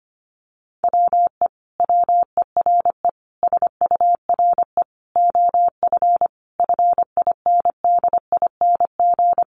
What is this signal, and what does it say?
Morse code, heard as a keyed tone, about 25 wpm, 700 Hz. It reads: WE WERE SURE OF FINDING
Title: Sentences from Top 1100 Words 25wpm